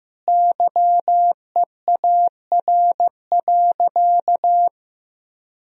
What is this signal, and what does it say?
Morse code, heard as a keyed tone, about 15 wpm, 700 Hz. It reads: YEAR.
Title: Sentences from Top 1000 Words - Review Part 1 - 15wpm